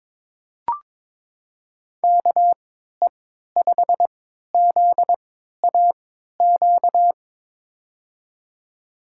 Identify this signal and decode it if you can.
Morse code, heard as a keyed tone, about 22 wpm, spaced as for 13 wpm, 700 Hz. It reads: KE5ZAQ
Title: USA Call Signs - 3x Character Spacing - 22wpm